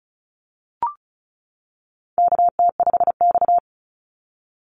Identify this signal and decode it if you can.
Morse code, heard as a keyed tone, about 36 wpm, 700 Hz. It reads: KT5X
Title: NA SOTA Activator Callsigns 35wpm